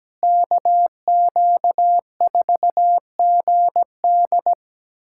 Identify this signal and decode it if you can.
Morse code, heard as a keyed tone, about 17 wpm, 700 Hz. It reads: KQ4GD
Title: USA Callsign DE Callsign - 5 Characters - 17wpm